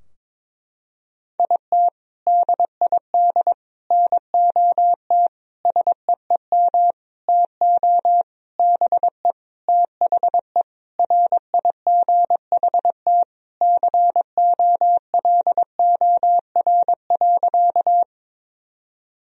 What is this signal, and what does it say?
Morse code, heard as a keyed tone, about 22 wpm, 700 Hz. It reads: IT DID NOT SEEM TO BE THE RIGHT COLOR.
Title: Sentences from Top 1000 Words - Review Part 2 - 22wpm